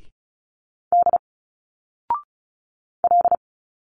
Morse code, heard as a keyed tone, about 36 wpm, 700 Hz. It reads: D L
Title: Single Letters 35wpm